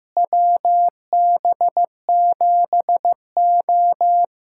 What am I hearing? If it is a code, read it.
Morse code, heard as a keyed tone, about 15 wpm, 700 Hz. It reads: WB7O
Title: USA Callsign DE Callsign - 4 Characters - 15wpm